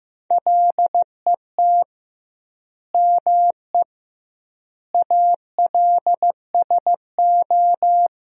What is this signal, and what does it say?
Morse code, heard as a keyed tone, about 16 wpm, 700 Hz. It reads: LET ME ALSO
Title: 2x Word-Spacing - Sentences from Top 400 Words - 15wpm